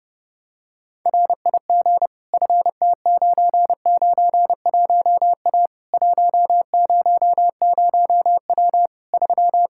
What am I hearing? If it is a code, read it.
Morse code, heard as a keyed tone, about 32 wpm, 700 Hz. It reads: RIG FT991A 100W 3
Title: K1USN SST QSO Practice 30wpm